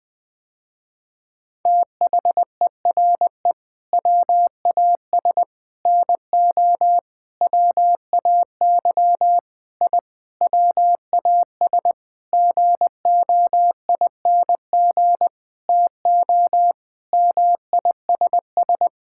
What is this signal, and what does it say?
Morse code, heard as a keyed tone, about 20 wpm, 700 Hz. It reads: THERE WAS NO WAY I WAS GOING TO MISS
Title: Sentences with 7-Word Phrases - Spoken Beginning - 20wpm